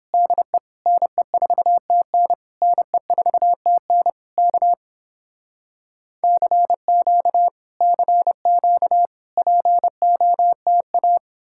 Morse code, heard as a keyed tone, about 29 wpm, 700 Hz. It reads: DE NE4TN NE4TN K CQ CQ POTA